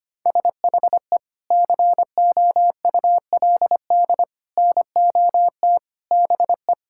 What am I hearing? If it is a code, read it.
Morse code, heard as a keyed tone, about 25 wpm, 700 Hz. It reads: SHE COULD NOT BE
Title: Sentences from Top 1000 Words - Review Part 6 - 25wpm